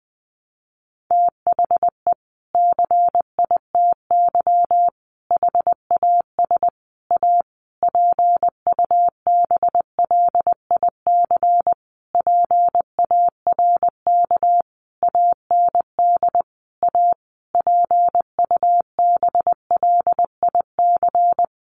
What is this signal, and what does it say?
Morse code, heard as a keyed tone, about 20 wpm, 700 Hz. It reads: THE CITY HAS A PUBLIC PARK AND A PUBLIC
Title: Sentences from Top 1300 Words 20wpm